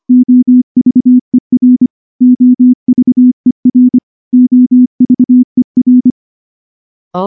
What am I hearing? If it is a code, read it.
Morse code, heard as a keyed tone, about 23 wpm, 250 Hz. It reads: OVER OVER OVER
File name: morse / morse-training-25WPM